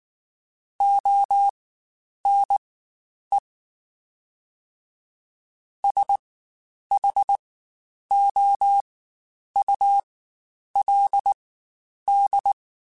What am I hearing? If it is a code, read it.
Morse code, heard as a keyed tone, about 19 wpm, spaced as for 9 wpm, 800 Hz. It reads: ONE SHOULD
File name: Quotes Spaced / Quotes Spaced-mp3s / Advice sp @18